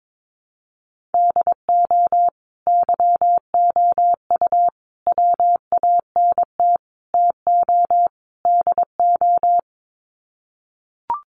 Morse code, heard as a keyed tone, about 22 wpm, 700 Hz. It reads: DO YOU WANT TO DO
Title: Sets of 5 Words Encore 22wpm